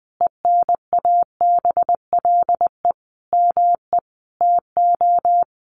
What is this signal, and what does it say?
Morse code, heard as a keyed tone, about 20 wpm, 700 Hz. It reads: ENABLE ME TO
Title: Sentences from Top 1500 Words 20wpm